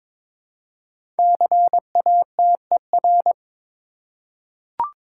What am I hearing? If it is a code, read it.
Morse code, heard as a keyed tone, about 22 wpm, 700 Hz. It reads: CATER
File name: 5 Letter Words - First 3 Letters Given - 22wpm